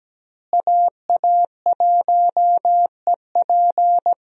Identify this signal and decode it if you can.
Morse code, heard as a keyed tone, about 17 wpm, 700 Hz. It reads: AA1EP
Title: USA Call Signs - 5 Characters - Repeated 2x - 17wpm